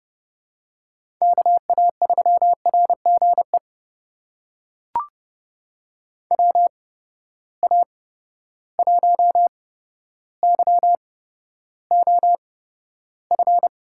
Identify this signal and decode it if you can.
Morse code, heard as a keyed tone, about 30 wpm, spaced as for 13 wpm, 700 Hz. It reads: KA3RGE WA1YOF
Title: USA Call Signs - 6 Characters - 8x Character Spacing - 30wpm